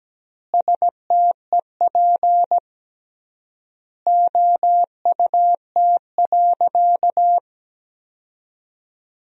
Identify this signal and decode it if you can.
Morse code, heard as a keyed tone, about 17 wpm, 700 Hz. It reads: STEP OUT.
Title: 3x Word-Spacing - Sentences from Top 400 Words - 17wpm